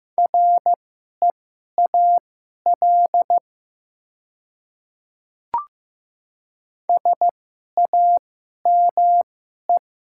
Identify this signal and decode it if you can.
Morse code, heard as a keyed tone, about 15 wpm, spaced as for 11 wpm, 700 Hz. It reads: REAL SAME
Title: 2x Character Spacing - Top 300 Words - 15wpm